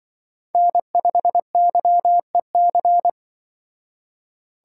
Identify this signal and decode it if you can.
Morse code, heard as a keyed tone, about 24 wpm, 700 Hz. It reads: N5YEC
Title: Speed Racing - USA 1x3 Call Signs 20wpm